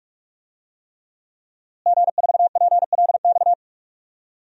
Morse code, heard as a keyed tone, about 46 wpm, 700 Hz. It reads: G4PLX
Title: DXCC-100 Call Signs - 5 Characters - Repeated 3x - 45wpm